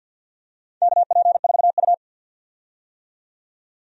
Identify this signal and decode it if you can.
Morse code, heard as a keyed tone, about 51 wpm, 700 Hz. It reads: KP4U
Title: DXCC-100 Call Signs - 4 Characters - 50wpm